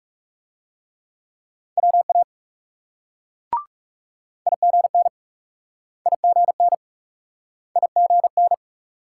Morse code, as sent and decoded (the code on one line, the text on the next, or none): .-- .- / .. --. -. / .. --. -. / .. --. -.
WA IGN IGN IGN